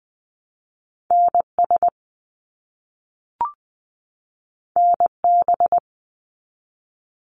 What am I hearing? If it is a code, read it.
Morse code, heard as a keyed tone, about 20 wpm, 700 Hz. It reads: NS NB
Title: Canadian Provinces and Territories 20wpm